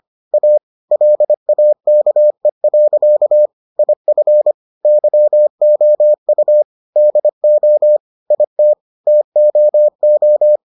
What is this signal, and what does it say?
Morse code, heard as a keyed tone, about 25 wpm, 600 Hz. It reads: A LAKE. IF YOU DO IT TOO